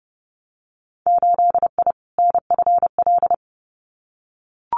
Morse code, heard as a keyed tone, about 32 wpm, 700 Hz. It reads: 8I NFL
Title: Winter Field Day Exchanges Encore 30wpm